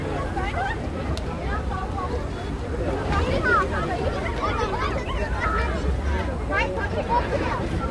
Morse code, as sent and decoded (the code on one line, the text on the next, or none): none